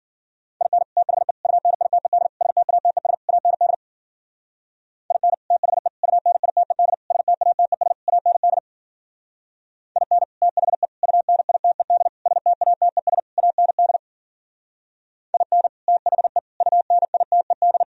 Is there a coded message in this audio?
Yes: Morse code from 0.6 to 17.8 s, heard as a keyed tone, about 50 wpm, 700 Hz. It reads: IN THE UNITED STATES AND IN THE UNITED STATES AND IN THE UNITED STATES AND IN THE UNITED